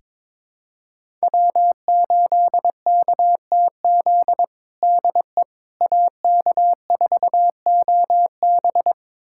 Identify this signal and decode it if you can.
Morse code, heard as a keyed tone, about 22 wpm, 700 Hz. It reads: W8KTZ DE AK4OB
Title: USA Callsign DE Callsign - 5 Characters - 22wpm